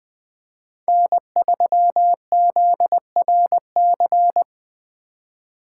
Morse code, heard as a keyed tone, about 19 wpm, 700 Hz. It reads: N3ZRC